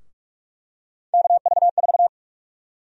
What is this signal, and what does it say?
Morse code, heard as a keyed tone, about 42 wpm, 700 Hz. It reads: KV4
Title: USA Call Sign Prefixes - 45wpm